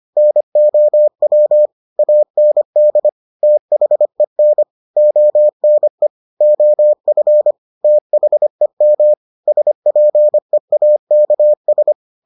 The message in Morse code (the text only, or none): NOW AND THEN ONE OF THEM SPEAKS